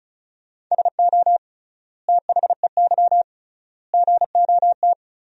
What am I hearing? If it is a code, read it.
Morse code, heard as a keyed tone, about 34 wpm, 700 Hz. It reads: SO THEY GOT
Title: 3x Word-Spacing - Sentences from Top 300 Words - 35wpm